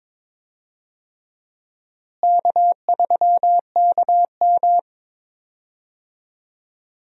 Morse code, heard as a keyed tone, about 21 wpm, 700 Hz. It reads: K3KM
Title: USA Call Signs - 4 Characters - Repeated 3x - 22wpm